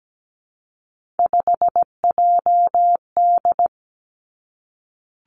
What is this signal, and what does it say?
Morse code, heard as a keyed tone, about 17 wpm, 700 Hz. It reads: 5JD
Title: Three Challenging Letter-Number Combinations 17wpm